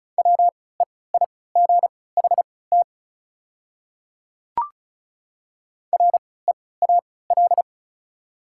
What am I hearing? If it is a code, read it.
Morse code, heard as a keyed tone, about 34 wpm, spaced as for 20 wpm, 690 Hz. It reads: WEIGHT REAL
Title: 3x Character Spacing - Top 500 Words - 35wpm